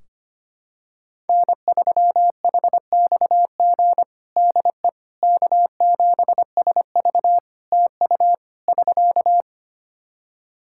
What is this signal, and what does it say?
Morse code, heard as a keyed tone, about 25 wpm, 700 Hz. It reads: N3HXG DE K7SV TU <SK>
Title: K1USN SST QSO Practice 25wpm